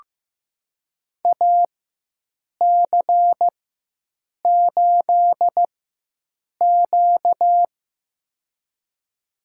Morse code, heard as a keyed tone, about 15 wpm, spaced as for 7 wpm, 700 Hz. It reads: AC8Q
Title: USA Call Signs - 4x Character Spacing - 15wpm